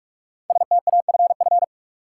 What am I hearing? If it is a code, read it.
Morse code, heard as a keyed tone, about 46 wpm, 700 Hz. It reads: STAFF
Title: Sentences from Top 1200 Words - New Word Drill Alt - 45wpm